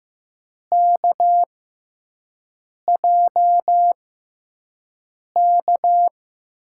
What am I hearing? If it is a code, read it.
Morse code, heard as a keyed tone, about 15 wpm, spaced as for 5 wpm, 700 Hz. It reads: KJK
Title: USA Call Signs - 5 Characters - 6x Character Spacing - 15wpm